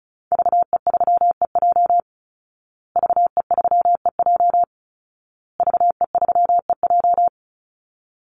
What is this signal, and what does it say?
Morse code, heard as a keyed tone, about 34 wpm, 700 Hz. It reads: VE3EJ VE3EJ VE3EJ
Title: Call Signs Repeated 2x - 35wpm